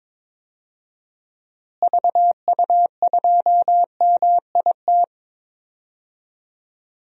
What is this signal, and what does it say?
Morse code, heard as a keyed tone, about 21 wpm, 700 Hz. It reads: VU2MIT
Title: DXCC-100 Call Signs - 6 Characters - Repeated 3x - 22wpm